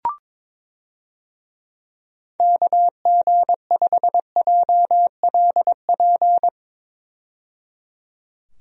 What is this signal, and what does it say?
Morse code, heard as a keyed tone, about 23 wpm, 700 Hz. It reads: KG5JLP
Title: USA Call Signs - 6 Characters - 22wpm